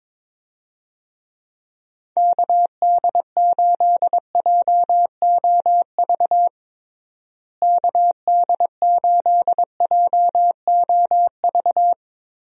Morse code, heard as a keyed tone, about 22 wpm, 700 Hz. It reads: KD8JOV KD8JOV
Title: USA Call Signs - 6 Characters - Repeated 3x - 22wpm